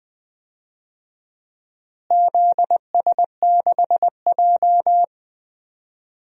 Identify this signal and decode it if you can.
Morse code, heard as a keyed tone, about 20 wpm, 700 Hz. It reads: ZS6J
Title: DXCC-100 Call Signs - 4 Characters - Repeated 3x - 20wpm